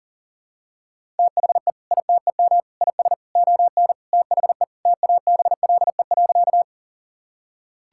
Morse code, heard as a keyed tone, about 40 wpm, 700 Hz. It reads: THE ITEM IS ON THE TABLE.
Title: Sentences from Top 1000 Words - Review Part 6 - 40wpm